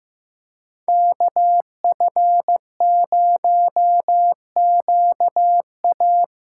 Morse code, heard as a keyed tone, about 15 wpm, 700 Hz. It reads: KF0QA
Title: USA Call Signs - 5 Characters - 8x Character Spacing - 15wpm